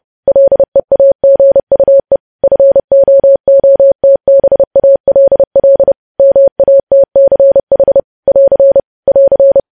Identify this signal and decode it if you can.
Morse code, heard as a keyed tone, about 30 wpm, 550 Hz. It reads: LEAGUE FOOTBALL MATCH <AR> <AR>
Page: News Headlines in Morse Code at 30 WPM